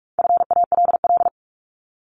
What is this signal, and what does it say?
Morse code, heard as a keyed tone, about 44 wpm, 700 Hz. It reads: FALL
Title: Top 500 Words Repeated 45wpm